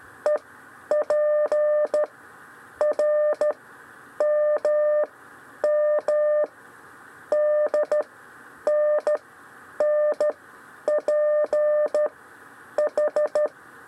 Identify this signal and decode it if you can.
Morse code, heard as a keyed tone, about 13 wpm, 600 Hz. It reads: E P R M M D N N P H
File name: 6 lezione cw 1